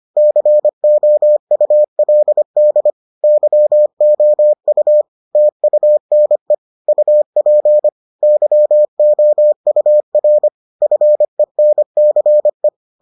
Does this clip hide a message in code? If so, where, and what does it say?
Yes: Morse code from 0.2 to 12.7 s, heard as a keyed tone, about 25 wpm, 600 Hz. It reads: COULD YOU TUNE UP YOUR FENCE